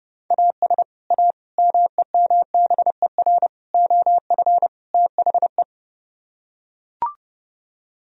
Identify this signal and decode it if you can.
Morse code, heard as a keyed tone, about 30 wpm, 700 Hz. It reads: AS A MEMBER OF THE